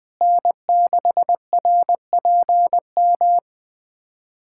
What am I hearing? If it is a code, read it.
Morse code, heard as a keyed tone, about 20 wpm, 700 Hz. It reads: N6RPM
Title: USA Callsign DE Callsign - 5 Characters - 20wpm